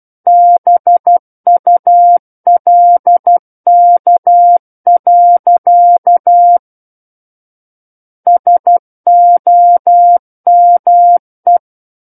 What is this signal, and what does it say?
Morse code, heard as a keyed tone, about 12 wpm, 700 Hz. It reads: BULK. SOME
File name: Book One / 03-Chapter 3 - 12 WPM